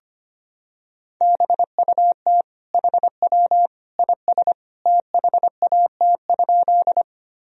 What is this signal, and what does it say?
Morse code, heard as a keyed tone, about 25 wpm, 700 Hz. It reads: BUT HW IS THAT?
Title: Sentences from Top 100 Words and Abbreviations 25wpm